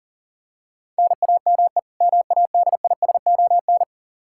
Morse code, heard as a keyed tone, about 40 wpm, 700 Hz. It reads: NAME MADISON
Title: Practice QSO - Exchange 1 - Beginning - 40wpm